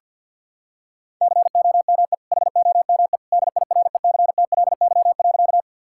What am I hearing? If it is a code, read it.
Morse code, heard as a keyed tone, about 52 wpm, 700 Hz. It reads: COME HOME DIRECTLY.
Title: Sentences from Top 1000 Words - Review Part 5 - 50wpm